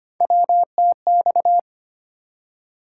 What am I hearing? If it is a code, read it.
Morse code, heard as a keyed tone, about 25 wpm, 700 Hz. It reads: WTX
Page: morse-code-ninja-practice-25wpm